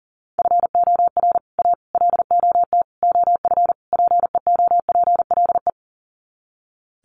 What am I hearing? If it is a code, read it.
Morse code, heard as a keyed tone, about 37 wpm, 700 Hz. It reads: FOR A LOT OF PEOPLE
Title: Sets of 5 Words Encore 40wpm